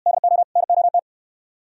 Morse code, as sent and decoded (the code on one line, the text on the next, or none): -.. --- / -. --- -
DO NOT